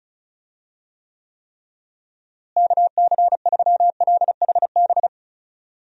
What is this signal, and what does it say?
Morse code, heard as a keyed tone, about 34 wpm, 700 Hz. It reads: KC3LHB
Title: USA Call Signs - 6 Characters - Repeated 3x - 35wpm